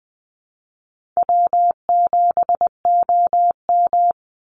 Morse code, heard as a keyed tone, about 20 wpm, 700 Hz. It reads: W7OM
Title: Easy Call Signs Repeated 2x - 20wpm